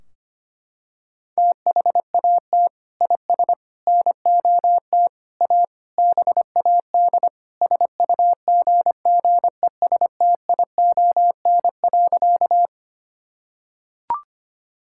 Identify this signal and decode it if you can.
Morse code, heard as a keyed tone, about 25 wpm, 700 Hz. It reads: THAT IS NOT A BAD SUGGESTION.
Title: Sentences from Top 1600 Words 25wpm